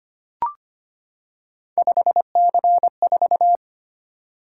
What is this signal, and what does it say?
Morse code, heard as a keyed tone, about 25 wpm, 700 Hz. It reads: 5C4
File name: Three Letter-Number Combinations - 25wpm